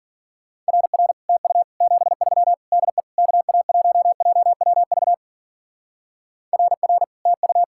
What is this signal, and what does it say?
Morse code, heard as a keyed tone, about 49 wpm, 700 Hz. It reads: RR TU 73 DE KA1JWV RR TU
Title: Speed-Racing Common QSO Phrases - 40wpm